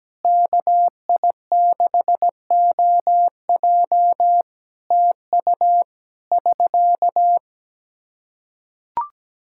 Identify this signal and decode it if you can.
Morse code, heard as a keyed tone, about 17 wpm, 700 Hz. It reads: KI6OJ TU <SK>
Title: Practice QSO - Exchange 4 - Ending - 17wpm